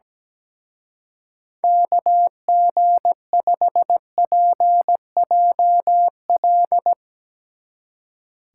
Morse code, heard as a keyed tone, about 18 wpm, 700 Hz. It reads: KG5PJL
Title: Speed Racing - USA 2x3 Call Signs 17wpm